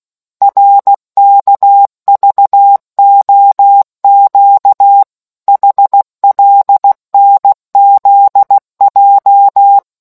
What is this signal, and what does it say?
Morse code, heard as a keyed tone, about 16 wpm, 800 Hz. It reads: RKVOQ HLNZJ